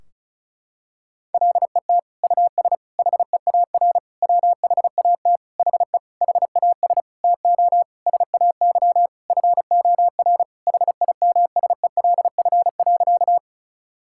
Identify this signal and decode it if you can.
Morse code, heard as a keyed tone, about 34 wpm, 700 Hz. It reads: LET US HEAR WHAT HE HAS TO SAY FOR HIMSELF.
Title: Sentences from Top 1600 Words 35wpm